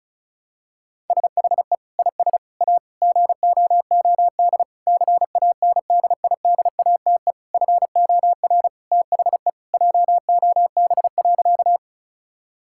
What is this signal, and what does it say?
Morse code, heard as a keyed tone, about 34 wpm, 700 Hz. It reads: SHE IS A GOOD CANDIDATE FOR THE JOB.
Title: Sentences from Top 1600 Words 35wpm